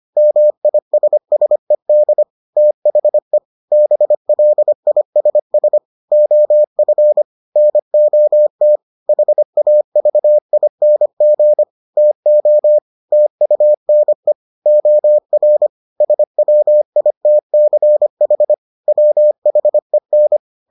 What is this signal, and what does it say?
Morse code, heard as a keyed tone, about 25 wpm, 600 Hz. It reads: MISSED THE BLISS OF NOT HAVING TO TUNE OR SWITCH WHEN